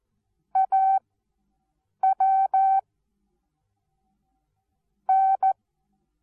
Morse code, heard as a keyed tone, about 14 wpm, 770 Hz. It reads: A W N
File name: Onlinelehrgang / morsen / DJ4UF Morsen19